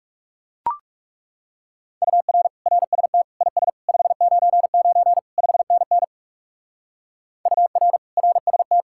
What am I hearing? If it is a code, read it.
Morse code, heard as a keyed tone, about 40 wpm, 700 Hz. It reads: UR RST IS 599 5NN UR RST